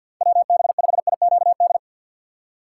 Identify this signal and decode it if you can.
Morse code, heard as a keyed tone, about 48 wpm, 700 Hz. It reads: WB5IQD
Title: USA Call Signs - 6 Characters - 3x Character Spacing - 50wpm